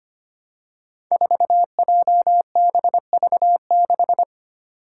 Morse code, heard as a keyed tone, about 25 wpm, 700 Hz. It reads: 4JBV6
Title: Five Challenging Letter-Number Combinations 25wpm